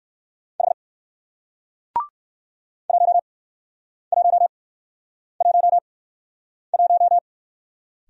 Morse code, heard as a keyed tone, about 54 wpm, 700 Hz. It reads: H 1 1 1 1
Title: Speed-Racing Single Letter-Number 45wpm